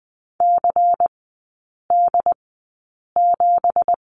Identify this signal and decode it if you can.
Morse code, heard as a keyed tone, about 20 wpm, spaced as for 8 wpm, 700 Hz. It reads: CD7
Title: 198 Lesson 32e - Letter k - Three Characters 20wpm